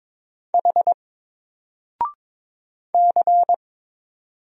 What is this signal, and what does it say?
Morse code, heard as a keyed tone, about 22 wpm, 700 Hz. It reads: H C
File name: Single-Letter-Number-22wpm